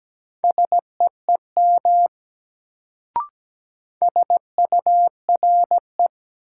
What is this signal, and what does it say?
Morse code, heard as a keyed tone, about 17 wpm, 700 Hz. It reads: SEEM SURE
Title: Sentences from Top 300 Words - New Word Drill - 17wpm